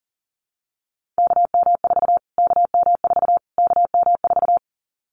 Morse code, heard as a keyed tone, about 37 wpm, 700 Hz. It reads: KM4 KM4 KM4
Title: Callsign Prefixes for Top 25 DXCC - Repeated 3x - 40wpm